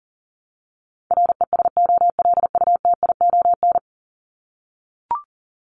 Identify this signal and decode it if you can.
Morse code, heard as a keyed tone, about 40 wpm, 700 Hz. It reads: RESOLUTION
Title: Top 1501-1750 Words 40wpm